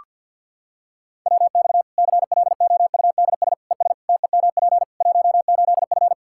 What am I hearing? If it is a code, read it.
Morse code, heard as a keyed tone, about 46 wpm, 700 Hz. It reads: WX CLOUDS ES TEMP 18F
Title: Practice QSO - Exchange 2 - Info - 50wpm